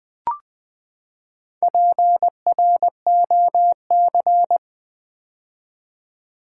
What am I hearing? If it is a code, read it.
Morse code, heard as a keyed tone, about 20 wpm, 700 Hz. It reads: PROC
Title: Build A Word - 4 Letters At A Time - 10-Letter Words - 20wpm